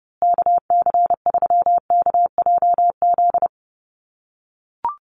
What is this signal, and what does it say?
Morse code, heard as a keyed tone, about 30 wpm, 700 Hz. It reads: KC3KJZ
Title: USA Call Signs Repeated 2x 30wpm